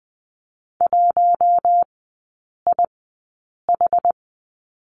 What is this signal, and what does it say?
Morse code, heard as a keyed tone, about 20 wpm, spaced as for 8 wpm, 700 Hz. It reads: 1IH
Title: 074 Lesson 13e - Letter L - Three Characters 20wpm